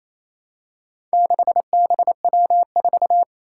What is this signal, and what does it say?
Morse code, heard as a keyed tone, about 28 wpm, 700 Hz. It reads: 6BW4
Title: Four Challenging Letter-Number Combinations 28wpm